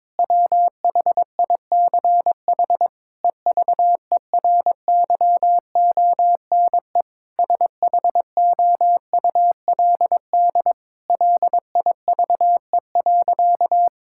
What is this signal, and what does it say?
Morse code, heard as a keyed tone, about 22 wpm, 700 Hz. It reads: WHICH EVERYONE SHOULD LIVE.
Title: Sentences from Top 1000 Words - Review Part 5 - 22wpm